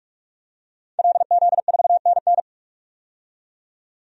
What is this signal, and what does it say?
Morse code, heard as a keyed tone, about 46 wpm, 700 Hz. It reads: LZ4NN
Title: DXCC-100 Call Signs - 5 Characters - 45wpm